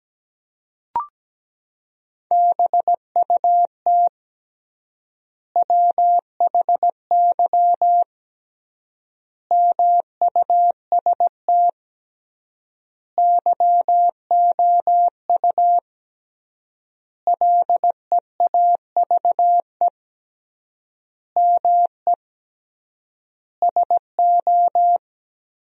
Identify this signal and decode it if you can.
Morse code, heard as a keyed tone, about 17 wpm, 700 Hz. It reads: BUT WHY MUST YOU LEAVE ME SO
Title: 3x Word-Spacing - Sentences from Top 400 Words - 17wpm